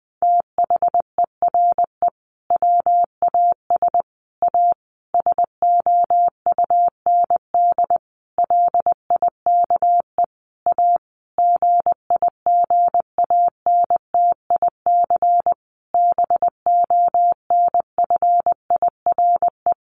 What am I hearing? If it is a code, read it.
Morse code, heard as a keyed tone, about 20 wpm, 700 Hz. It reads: THERE WAS A SOUND LIKE A GIGANTIC BONFIRE
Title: A Sound Of Thunder 20wpm